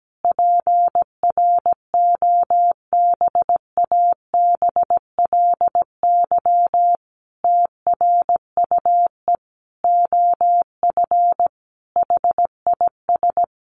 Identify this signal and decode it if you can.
Morse code, heard as a keyed tone, about 17 wpm, 700 Hz. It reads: PROBABLY TRUE OF HIS
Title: Sentences from Top 700 Words 17wpm